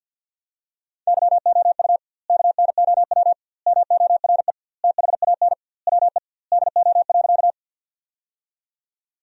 Morse code, heard as a keyed tone, about 50 wpm, 700 Hz. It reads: YOU KNOW MORE THAN WE DO.
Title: 2x Word-Spacing - Sentences from Top 100 Words - 50wpm